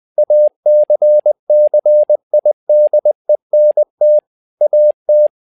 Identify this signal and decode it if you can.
Morse code, heard as a keyed tone, about 20 wpm, 600 Hz. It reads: ACCIDENT AT
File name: learning center / tower trouble / tower trouble 20wpm